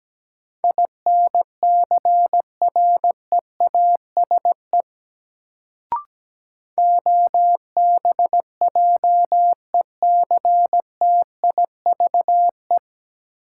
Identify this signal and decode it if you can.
Morse code, heard as a keyed tone, about 17 wpm, 700 Hz. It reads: INCREASE OBJECTIVE
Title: Sentences from Top 1200 Words - New Word Drill Alt - 17wpm